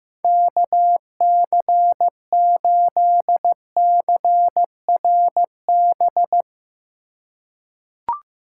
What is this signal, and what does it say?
Morse code, heard as a keyed tone, about 15 wpm, 700 Hz. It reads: KC8CRB
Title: USA Call Signs - 6 Characters - 4x Character Spacing - 15wpm